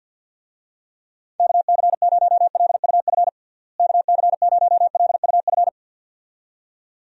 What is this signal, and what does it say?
Morse code, heard as a keyed tone, about 51 wpm, 700 Hz. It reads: KC0LUF KC0LUF
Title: USA Call Signs - 6 Characters - Repeated 2x - 50wpm